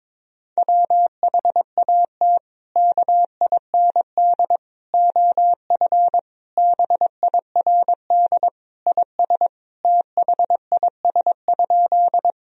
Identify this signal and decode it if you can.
Morse code, heard as a keyed tone, about 22 wpm, 700 Hz. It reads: WHAT KIND OF BIRD IS THIS?
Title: Sentences from Top 1000 Words - Review Part 2 - 22wpm